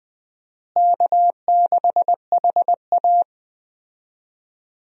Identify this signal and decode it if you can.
Morse code, heard as a keyed tone, about 21 wpm, 700 Hz. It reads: K6HA
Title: USA Call Signs - 4 Characters - Repeated 3x - 20wpm